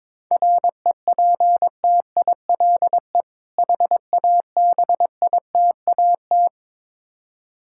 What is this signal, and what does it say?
Morse code, heard as a keyed tone, about 22 wpm, 690 Hz. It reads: REPTILE HABITAT
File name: Difficult Words with Hints - 22wpm